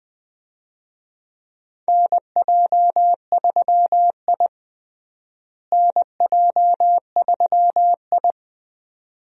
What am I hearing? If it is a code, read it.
Morse code, heard as a keyed tone, about 20 wpm, 700 Hz. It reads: NJ3I NJ3I
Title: USA Call Signs - 4 Characters - Repeated 2x - 20wpm